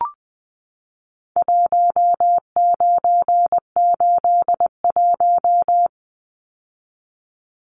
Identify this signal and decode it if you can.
Morse code, heard as a keyed tone, about 20 wpm, 700 Hz. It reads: 1981
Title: Numbers - Contest Style - 4-Digit 20wpm